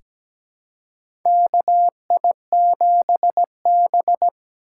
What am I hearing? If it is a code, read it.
Morse code, heard as a keyed tone, about 17 wpm, 700 Hz. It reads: KI7B
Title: USA Call Signs - 4 Characters - 3x Character Spacing - 17wpm